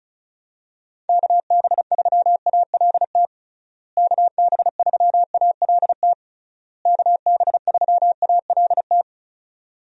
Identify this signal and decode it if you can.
Morse code, heard as a keyed tone, about 36 wpm, 700 Hz. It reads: KB3ALT KB3ALT KB3ALT
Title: USA Call Signs - 6 Characters - Repeated 3x - 35wpm